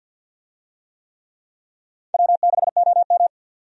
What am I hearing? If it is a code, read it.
Morse code, heard as a keyed tone, about 53 wpm, 700 Hz. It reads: W6OM
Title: USA Call Signs - 4 Characters - 50wpm